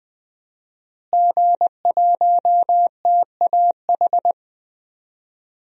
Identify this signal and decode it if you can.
Morse code, heard as a keyed tone, about 20 wpm, 700 Hz. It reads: G1TAH
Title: DXCC-100 Call Signs - 5 Characters - Repeated 3x - 20wpm